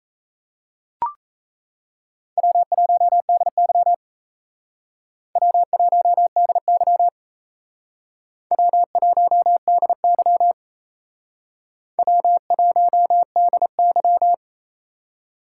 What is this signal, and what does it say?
Morse code, heard as a keyed tone, about 36 wpm, 700 Hz. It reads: W1DY W1DY W1DY W1DY